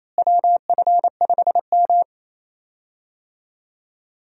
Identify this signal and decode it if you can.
Morse code, heard as a keyed tone, about 28 wpm, 700 Hz. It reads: WF5M
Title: USA Call Signs - 4 Characters - Repeated 3x - 28wpm